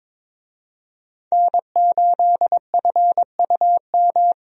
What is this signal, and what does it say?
Morse code, heard as a keyed tone, about 22 wpm, 700 Hz. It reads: N8FUM
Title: USA Call Signs - 5 Characters - Repeated 2x - 22wpm